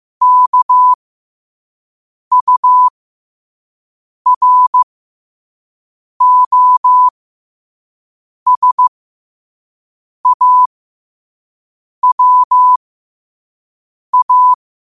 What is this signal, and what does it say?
Morse code, heard as a keyed tone, about 15 wpm, spaced as for 5 wpm, 1000 Hz. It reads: KUROSAWA
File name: qotdfiles / 5wpm